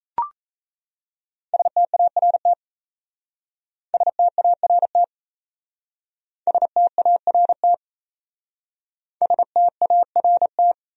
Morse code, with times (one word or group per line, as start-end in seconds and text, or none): 1.5-2.5 START
3.9-5.0 START
6.5-7.7 START
9.2-10.7 START